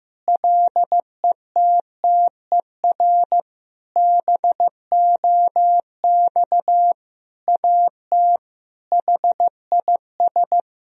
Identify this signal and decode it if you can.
Morse code, heard as a keyed tone, about 15 wpm, 700 Hz. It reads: LETTER BOX AT HIS
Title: Sentences from Top 1000 Words - Review Part 2 - 15wpm